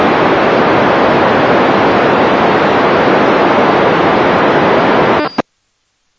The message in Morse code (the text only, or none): none